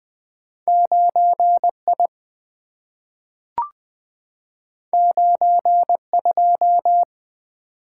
Morse code, heard as a keyed tone, about 20 wpm, 700 Hz. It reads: 9I 92
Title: Two Letter-Number Combinations - 20wpm